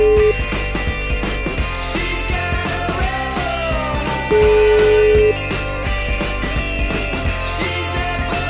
Ringback tone (Europe, 425 Hz), with noise.